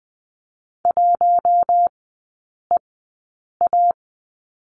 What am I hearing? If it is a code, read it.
Morse code, heard as a keyed tone, about 20 wpm, spaced as for 8 wpm, 700 Hz. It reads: 1EA
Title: 041 Lesson 8e - Number 1 - Three Characters 20wpm